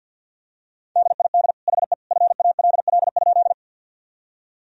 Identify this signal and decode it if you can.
Morse code, heard as a keyed tone, about 52 wpm, 700 Hz. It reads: DID HE FALL?
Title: Sentences from Top 1000 Words - Review Part 2 - 50wpm